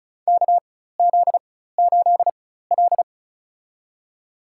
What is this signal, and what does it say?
Morse code, heard as a keyed tone, about 36 wpm, spaced as for 16 wpm, 700 Hz. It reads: KZ8L
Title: USA Call Signs - 4x Character Spacing - 35wpm